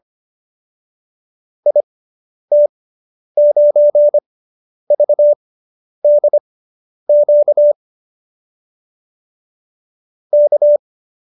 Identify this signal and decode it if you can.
Morse code, heard as a keyed tone, about 25 wpm, spaced as for 10 wpm, 570 Hz. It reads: IT9VDQ K